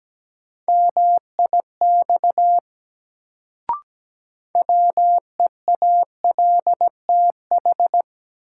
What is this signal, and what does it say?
Morse code, heard as a keyed tone, about 17 wpm, 700 Hz. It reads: MIX WEALTH
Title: Sentences from Top 1000 Words - Review Part 4 - New Word Drill - 17wpm